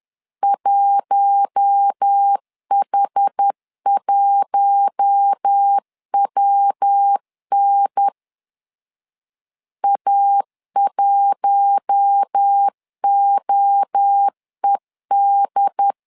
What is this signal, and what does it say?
Morse code, heard as a keyed tone, about 10 wpm, 800 Hz. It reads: 1H1WN A1OED
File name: LEZIONI / 4 Q1 10wpm 2448